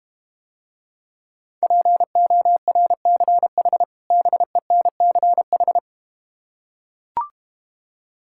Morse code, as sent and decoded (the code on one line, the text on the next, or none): .--. --- .-. -.-. .... / -... -. -.-. ....
PORCH BENCH